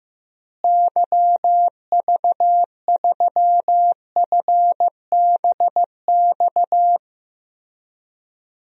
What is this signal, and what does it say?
Morse code, heard as a keyed tone, about 16 wpm, 700 Hz. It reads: YV3FBX
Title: DXCC-100 Call Signs - 6 Characters - Repeated 3x - 15wpm